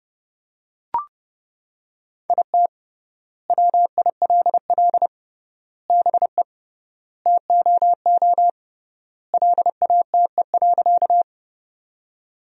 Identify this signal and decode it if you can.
Morse code, heard as a keyed tone, about 30 wpm, 700 Hz. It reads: IT WILL BE TOO LATE.
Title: 3x Word-Spacing - Sentences from Top 300 Words - 30wpm